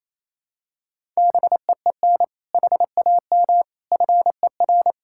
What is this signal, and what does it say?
Morse code, heard as a keyed tone, about 28 wpm, 700 Hz. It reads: BEEN HAM FER
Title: K1USN SST QSO Practice 28wpm